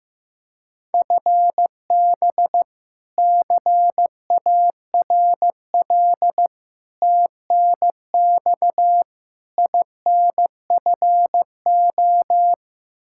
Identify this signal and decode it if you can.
Morse code, heard as a keyed tone, about 15 wpm, 700 Hz. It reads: FB CARL TNX INFO
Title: Practice QSO - Exchange 3 - More Info - 15wpm